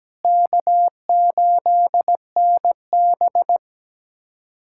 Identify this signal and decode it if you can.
Morse code, heard as a keyed tone, about 17 wpm, 700 Hz. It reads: K8NB
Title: USA Callsign DE Callsign - 4 Characters - 17wpm